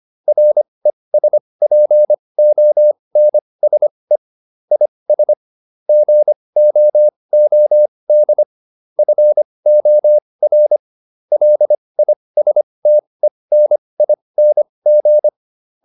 Morse code, heard as a keyed tone, about 25 wpm, spaced as for 20 wpm, 600 Hz. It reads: RESPONSE IS GOOD FOR LISTENING